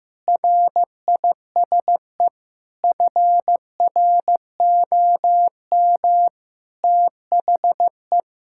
Morse code, heard as a keyed tone, about 15 wpm, 700 Hz. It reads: RISE FROM THE
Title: Sentences from Top 1000 Words - Review Part 6 - 15wpm